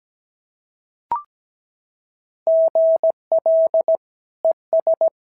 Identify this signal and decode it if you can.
Morse code, heard as a keyed tone, about 16 wpm, 650 Hz. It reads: GL ES
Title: SOTA QSO Practice 17wpm